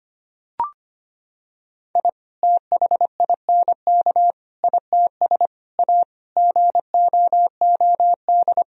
Morse code, heard as a keyed tone, about 25 wpm, 700 Hz. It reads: I THINK ITS A GOOD
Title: Sentences from Top 1000 Words - Review Part 1 - 25wpm